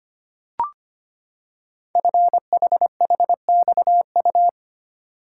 Morse code, heard as a keyed tone, about 25 wpm, 700 Hz. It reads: FHHXU
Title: Code Groups - 5 Letters - 25wpm